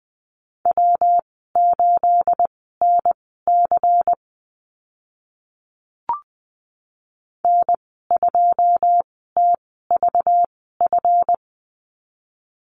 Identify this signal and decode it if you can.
Morse code, heard as a keyed tone, about 20 wpm, spaced as for 14 wpm, 700 Hz. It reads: W8NC N2TVF
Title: 219 Lesson 35g - Number 0 - Call Signs 20wpm